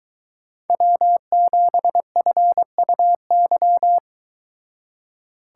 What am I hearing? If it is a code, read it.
Morse code, heard as a keyed tone, about 23 wpm, 700 Hz. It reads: W7FUY